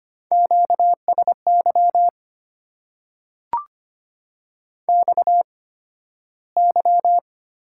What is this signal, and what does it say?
Morse code, heard as a keyed tone, about 25 wpm, spaced as for 13 wpm, 700 Hz. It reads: QSY XY
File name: Top 90 QSO Elements - 8x Character Spacing - 25wpm